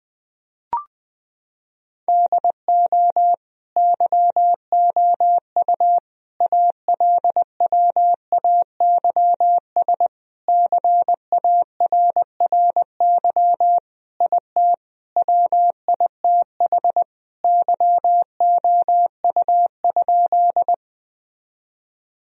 Morse code, heard as a keyed tone, about 21 wpm, 700 Hz. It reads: DO YOU ALWAYS CARRY IT WITH YOU?
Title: Sentences from Top 1000 Words - Review Part 1 - 20wpm